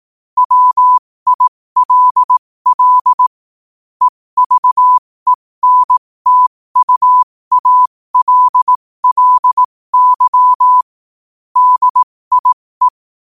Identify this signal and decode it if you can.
Morse code, heard as a keyed tone, about 18 wpm, spaced as for 15 wpm, 1000 Hz. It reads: WILL EVENTUALLY DIE
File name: 15wpm